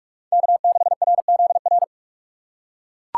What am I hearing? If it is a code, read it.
Morse code, heard as a keyed tone, about 41 wpm, 700 Hz. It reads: K6RZR